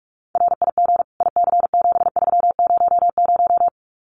Morse code, heard as a keyed tone, about 44 wpm, 700 Hz. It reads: RIG IC7300
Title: Common QSO Phrases Encore 45wpm